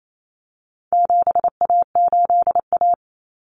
Morse code, heard as a keyed tone, about 28 wpm, 700 Hz. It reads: 7A8A
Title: Common Cut Numbers - 4-Digit 28wpm